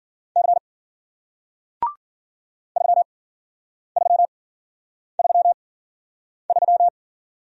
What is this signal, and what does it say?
Morse code, heard as a keyed tone, about 52 wpm, spaced as for 9 wpm, 700 Hz. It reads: C 3333